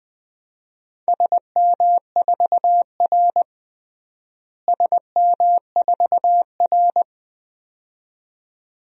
Morse code, heard as a keyed tone, about 20 wpm, 700 Hz. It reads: SM4R SM4R
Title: DXCC-100 Call Signs - 4 Characters - Repeated 2x - 20wpm